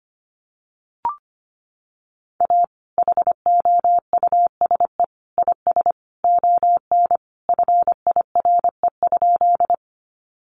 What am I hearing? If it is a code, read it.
Morse code, heard as a keyed tone, about 25 wpm, 700 Hz. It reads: A HOUSE IS ON FIRE?
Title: Sentences from Top 400 Words 25wpm